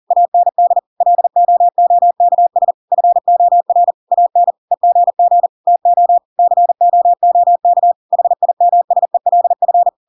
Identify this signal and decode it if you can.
Morse code, heard as a keyed tone, about 40 wpm, 700 Hz. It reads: AND LOOKS FOR AN EGG TO COOK HIMSELF
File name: Pinocchio / Pinocchio ch5 40 40 ik1hge